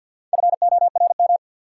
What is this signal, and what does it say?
Morse code, heard as a keyed tone, about 50 wpm, 700 Hz. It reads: FORM
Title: Sentences from Top 1000 Words - Review Part 1 - New Word Drill - 50wpm